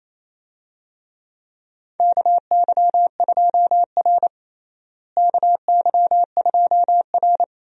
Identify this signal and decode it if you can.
Morse code, heard as a keyed tone, about 28 wpm, 700 Hz. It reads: KY2R KY2R
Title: USA Call Signs - 4 Characters - Repeated 3x - 28wpm